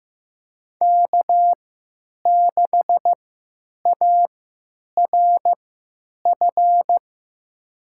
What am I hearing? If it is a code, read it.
Morse code, heard as a keyed tone, about 15 wpm, spaced as for 9 wpm, 700 Hz. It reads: K6ARF
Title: USA Call Signs - 5 Characters - 3x Character Spacing - 15wpm